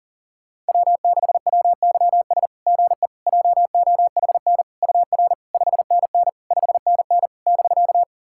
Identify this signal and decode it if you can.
Morse code, heard as a keyed tone, about 40 wpm, 690 Hz. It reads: W6WYS GE JOHN UR 5NN 5NN <BK>
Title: SOTA QSO Practice 40wpm